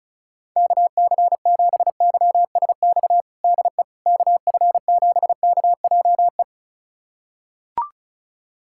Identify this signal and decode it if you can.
Morse code, heard as a keyed tone, about 36 wpm, 700 Hz. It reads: KC7YSX DE KF7KJE
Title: USA Callsign DE Callsign - 6 Characters - 35wpm